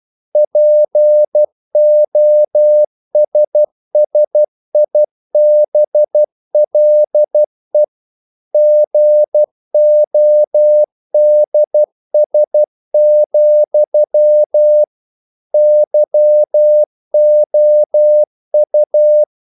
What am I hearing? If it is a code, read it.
Morse code, heard as a keyed tone, about 12 wpm, 600 Hz. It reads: POSSIBLE GODS, YOU
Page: English quotations in CW (telegraphy - morse)